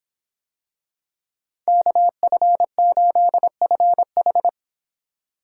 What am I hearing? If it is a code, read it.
Morse code, heard as a keyed tone, about 26 wpm, 700 Hz. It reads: KF8FH